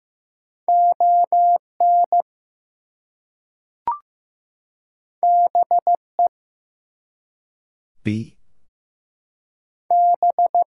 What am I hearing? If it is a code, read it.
Morse code, heard as a keyed tone, about 15 wpm, 700 Hz. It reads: ON BE B
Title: Two Letter Words - 15wpm